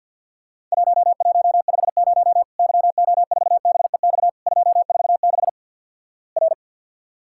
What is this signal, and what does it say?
Morse code, heard as a keyed tone, about 48 wpm, 700 Hz. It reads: 1150 YO4BEX 246 R
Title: Worked All Europe DX Contest - QTC Exchange - 50wpm